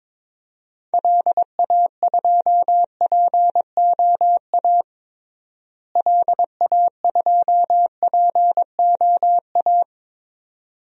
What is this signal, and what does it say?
Morse code, heard as a keyed tone, about 21 wpm, 700 Hz. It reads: LA2POA LA2POA
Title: DXCC-100 Call Signs - 6 Characters - Repeated 2x - 22wpm